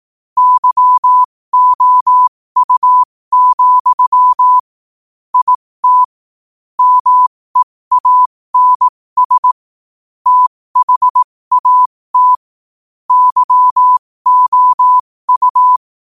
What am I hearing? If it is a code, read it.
Morse code, heard as a keyed tone, about 18 wpm, spaced as for 15 wpm, 1000 Hz. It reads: YOU, IT MEANS THAT YOU
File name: qotdfiles / 15wpm